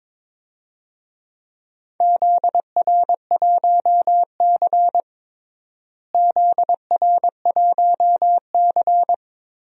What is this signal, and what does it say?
Morse code, heard as a keyed tone, about 22 wpm, 700 Hz. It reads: ZR1C ZR1C
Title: DXCC-100 Call Signs - 4 Characters - Repeated 3x - 22wpm